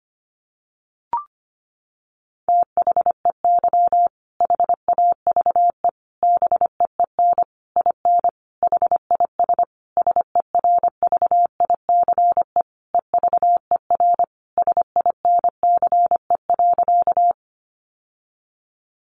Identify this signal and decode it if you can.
Morse code, heard as a keyed tone, about 25 wpm, 700 Hz. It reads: THEY HAVE BEEN IN HIS SERVICE EVER SINCE.
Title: Sentences from Top 700 Words 25wpm